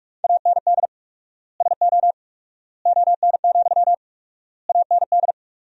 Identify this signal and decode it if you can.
Morse code, heard as a keyed tone, about 46 wpm, 700 Hz. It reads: AND SO ON, AND
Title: 4x Word-Spacing - Sentences from Top 100 Words - 45wpm